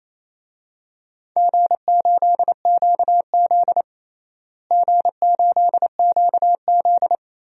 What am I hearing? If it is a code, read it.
Morse code, heard as a keyed tone, about 28 wpm, 700 Hz. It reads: G8QZ G8QZ
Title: DXCC-100 Call Signs - 4 Characters - Repeated 3x - 28wpm